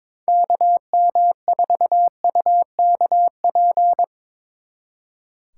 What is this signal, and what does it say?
Morse code, heard as a keyed tone, about 24 wpm, 700 Hz. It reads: KM4UKP
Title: Speed Racing - USA 2x3 Call Signs 22wpm